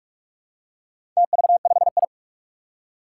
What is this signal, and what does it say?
Morse code, heard as a keyed tone, about 46 wpm, 700 Hz. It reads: TV5I
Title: Four Challenging Letter-Number Combinations 45wpm